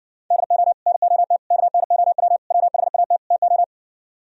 Morse code, heard as a keyed tone, about 51 wpm, 700 Hz. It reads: DO NOT KNOW WHAT TO